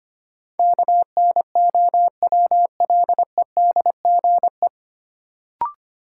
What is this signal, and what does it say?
Morse code, heard as a keyed tone, about 25 wpm, 700 Hz. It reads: KNOWLEDGE
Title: Sentences from Top 600 Words - New Word Drill - 25wpm